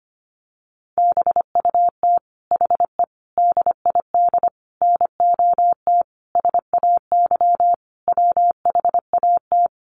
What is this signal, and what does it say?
Morse code, heard as a keyed tone, about 25 wpm, 700 Hz. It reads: BUT HE DID NOT SAY WHAT
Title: Sentences from Top 600 Words 25wpm